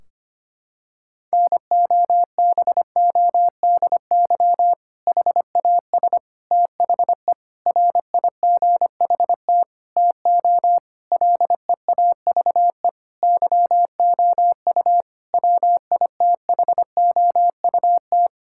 Morse code, heard as a keyed tone, about 25 wpm, 700 Hz. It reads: NOBODY HAS THE RIGHT TO LEAVE YOU WITHOUT
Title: Sentences from Top 1600 Words 25wpm